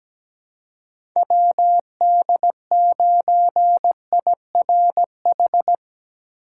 Morse code, heard as a keyed tone, about 17 wpm, 700 Hz. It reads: WD9IRH
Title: USA Call Signs - 3x Character Spacing - 17wpm